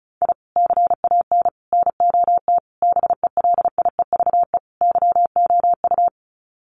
Morse code, heard as a keyed tone, about 34 wpm, 700 Hz. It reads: I CAN NOT BELIEVE YOU
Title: Sets of 5 Words Encore 35wpm